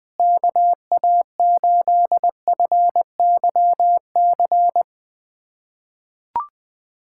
Morse code, heard as a keyed tone, about 20 wpm, 700 Hz. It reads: KA8FYC
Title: USA Callsign DE Callsign - 6 Characters - 20wpm